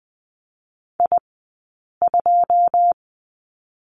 Morse code, heard as a keyed tone, about 20 wpm, spaced as for 8 wpm, 700 Hz. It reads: I2
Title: 080 Lesson 14d - Number 2 - Two Characters 20wpm